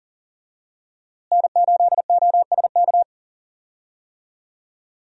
Morse code, heard as a keyed tone, about 40 wpm, 700 Hz. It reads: N8OSK
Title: USA Call Signs - 5 Characters - Repeated 2x - 40wpm